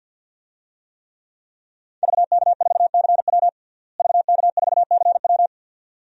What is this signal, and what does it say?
Morse code, heard as a keyed tone, about 51 wpm, 700 Hz. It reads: VK4CW VK4CW
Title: DXCC-100 Call Signs - 5 Characters - Repeated 2x - 50wpm